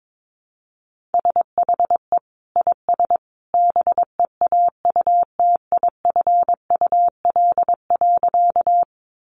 Morse code, heard as a keyed tone, about 22 wpm, 700 Hz. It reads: SHE IS BEAUTIFUL.
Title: Sentences from Top 900 Words 22wpm